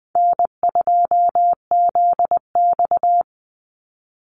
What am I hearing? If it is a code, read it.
Morse code, heard as a keyed tone, about 20 wpm, 700 Hz. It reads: N2ZX
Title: Call Signs Repeated 2x - 20wpm